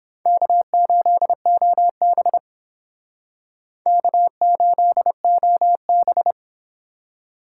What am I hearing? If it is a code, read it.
Morse code, heard as a keyed tone, about 28 wpm, 700 Hz. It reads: K8OB K8OB